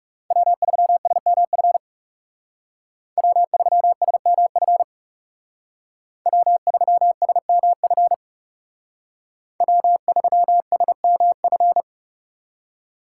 Morse code, heard as a keyed tone, about 39 wpm, 700 Hz. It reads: W3SMF W3SMF W3SMF W3SMF